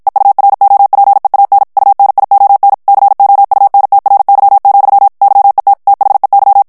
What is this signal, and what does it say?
Morse code, heard as a keyed tone, about 56 wpm, 800 Hz. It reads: EUROPEAN UNION COUNTRY, YET THEY
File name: morse mp3 / ANGLAIS texte clair 1 vitesse 50 mots minute